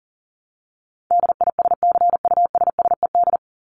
Morse code, heard as a keyed tone, about 40 wpm, 700 Hz. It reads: DISCUSSED
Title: ED Suffix - Top 100 Words with Base Words 40wpm